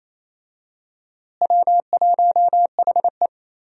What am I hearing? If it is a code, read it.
Morse code, heard as a keyed tone, about 28 wpm, 700 Hz. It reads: W1HE
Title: USA Call Signs - 4 Characters - Repeated 3x - 28wpm